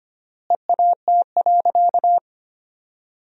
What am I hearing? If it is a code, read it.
Morse code, heard as a keyed tone, about 25 wpm, 700 Hz. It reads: EAT.
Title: 3x Word-Spacing - Sentences from Top 300 Words - 25wpm